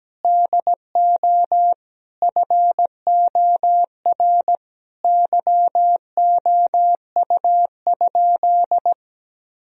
Morse code, heard as a keyed tone, about 17 wpm, 700 Hz. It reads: DO FOR YOU?
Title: Sentences from Top 1000 Words - Review Part 2 - 17wpm